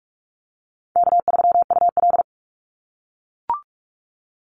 Morse code, heard as a keyed tone, about 44 wpm, 700 Hz. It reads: K3UL
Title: Call Signs - 45wpm